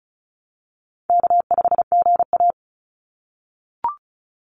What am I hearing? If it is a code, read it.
Morse code, heard as a keyed tone, about 36 wpm, 700 Hz. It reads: K5GA
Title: Call Signs - 35wpm